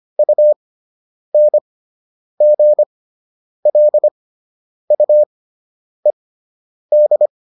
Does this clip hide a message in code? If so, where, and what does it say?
Yes: Morse code from 0.2 to 7.3 s, heard as a keyed tone, about 25 wpm, spaced as for 9 wpm, 600 Hz. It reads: UNGLUED